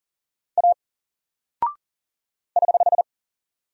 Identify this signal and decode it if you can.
Morse code, heard as a keyed tone, about 39 wpm, 700 Hz. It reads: A <HH>
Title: Single Character (Letter:Number:Punctuation:Prosign) 40wpm